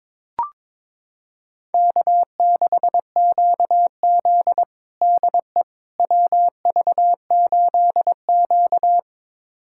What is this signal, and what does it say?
Morse code, heard as a keyed tone, about 23 wpm, 700 Hz. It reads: K6QZ DE WV8Q
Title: USA Callsign DE Callsign - 4 Characters - 22wpm